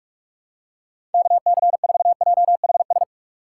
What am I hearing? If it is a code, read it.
Morse code, heard as a keyed tone, about 46 wpm, 700 Hz. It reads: KC4JHS
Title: USA Call Signs - 6 Characters - 2x Character Spacing - 45wpm